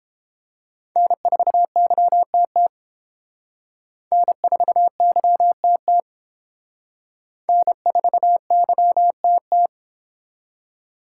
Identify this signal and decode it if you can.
Morse code, heard as a keyed tone, about 29 wpm, 700 Hz. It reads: N4YTT N4YTT N4YTT